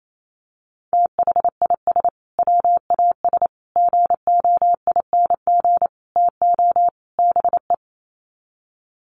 Morse code, heard as a keyed tone, about 28 wpm, 700 Hz. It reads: THIS WAS GOING TO BE
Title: Sets of 5 Words Encore2 28wpm